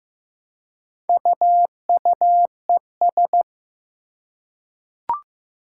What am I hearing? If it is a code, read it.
Morse code, heard as a keyed tone, about 15 wpm, 700 Hz. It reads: UUES
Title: Four Challenging Letter-Number Combinations 15wpm